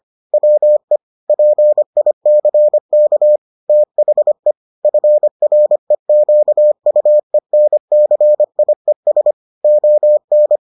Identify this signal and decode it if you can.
Morse code, heard as a keyed tone, about 25 wpm, 600 Hz. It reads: WE PICK THE FREQUENCIES ON